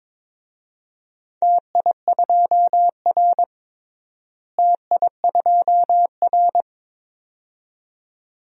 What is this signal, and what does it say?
Morse code, heard as a keyed tone, about 22 wpm, 700 Hz. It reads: TI2R TI2R
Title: DXCC-100 Call Signs - 4 Characters - Repeated 2x - 22wpm